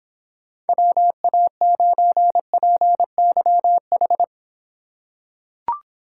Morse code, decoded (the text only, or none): WA9PYH